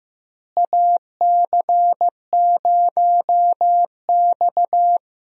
Morse code, heard as a keyed tone, about 15 wpm, 700 Hz. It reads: AC0X
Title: USA Call Signs - 4 Characters - Repeated 3x - 15wpm